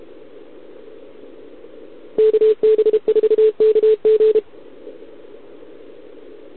Morse code, heard as a keyed tone, about 32 wpm, 430 Hz. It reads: KB4KG
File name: ARCHIV / OKDX2020 / OK1RIb